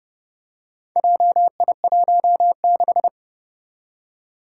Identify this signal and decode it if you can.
Morse code, heard as a keyed tone, about 30 wpm, 700 Hz. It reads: JI16
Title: Four Challenging Letter-Number Combinations 30wpm